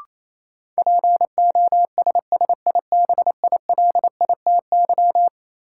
Morse code, heard as a keyed tone, about 28 wpm, 700 Hz. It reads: POSSIBILITY
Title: Sets of 3 Words - New Word Drill - 28wpm